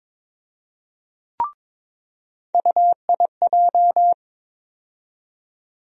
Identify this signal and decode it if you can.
Morse code, heard as a keyed tone, about 21 wpm, 700 Hz. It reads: UIJ
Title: USA Call Sign Postfixes - 3 Letters 22wpm